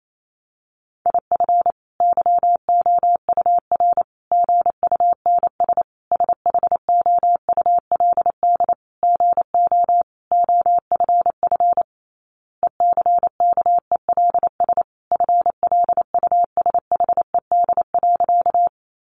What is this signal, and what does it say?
Morse code, heard as a keyed tone, about 28 wpm, 700 Hz. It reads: IF YOUR GUNS SHOULD GO OFF ECKELS FLUSHED.
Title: A Sound Of Thunder 28wpm